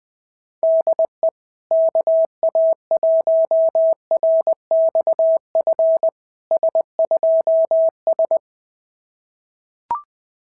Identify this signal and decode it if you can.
Morse code, heard as a keyed tone, about 21 wpm, 650 Hz. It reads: DE KA1RXF S2S
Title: SOTA QSO Practice 20wpm